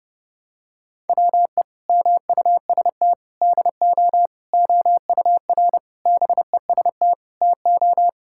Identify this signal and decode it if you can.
Morse code, heard as a keyed tone, about 29 wpm, 700 Hz. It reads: WE MUST DO OUR BEST TO
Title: Sentences from Top 1000 Words - Review Part 5 - 30wpm